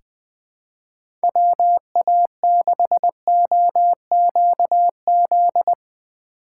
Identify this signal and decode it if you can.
Morse code, heard as a keyed tone, about 21 wpm, 700 Hz. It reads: WA6OQZ
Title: USA Call Signs - 6 Characters - 8x Character Spacing - 20wpm